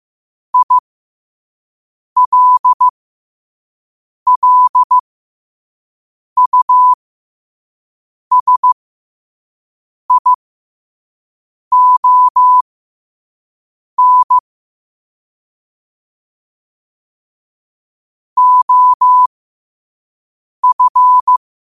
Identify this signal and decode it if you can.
Morse code, heard as a keyed tone, about 15 wpm, spaced as for 5 wpm, 1000 Hz. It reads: ILLUSION OF
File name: qotdfiles / 5wpm